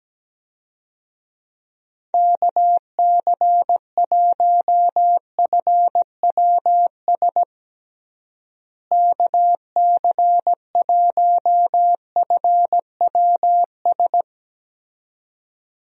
Morse code, heard as a keyed tone, about 17 wpm, 700 Hz. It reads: KC1FWS KC1FWS
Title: USA Call Signs - 6 Characters - Repeated 2x - 17wpm